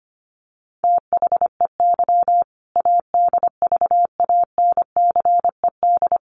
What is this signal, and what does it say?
Morse code, heard as a keyed tone, about 25 wpm, 700 Hz. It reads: THEY ADVANCED
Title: Sentences from Top 1200 Words 25wpm